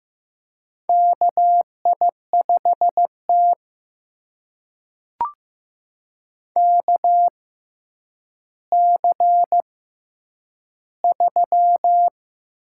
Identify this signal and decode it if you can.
Morse code, heard as a keyed tone, about 15 wpm, spaced as for 9 wpm, 700 Hz. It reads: KI5T KC3
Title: USA Call Signs - 4 Characters - 6x Character Spacing - 15wpm